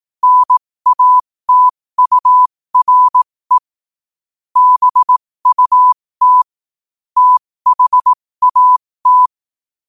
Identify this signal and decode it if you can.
Morse code, heard as a keyed tone, about 18 wpm, spaced as for 15 wpm, 1000 Hz. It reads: NATURE BUT THAT
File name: qotdfiles / 15wpm